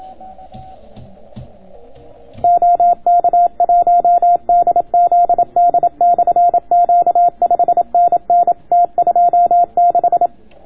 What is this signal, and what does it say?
Morse code, heard as a keyed tone, about 27 wpm, 700 Hz. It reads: OK1DZD/Q 5NN T26